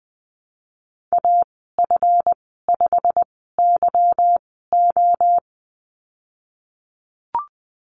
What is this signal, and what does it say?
Morse code, heard as a keyed tone, about 20 wpm, spaced as for 15 wpm, 700 Hz. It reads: AF5YO
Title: 173 Lesson 28g - Number 9 - Call Signs 20wpm